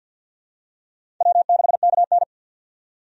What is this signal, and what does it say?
Morse code, heard as a keyed tone, about 53 wpm, 700 Hz. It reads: W6KN
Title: USA Call Signs - 4 Characters - 50wpm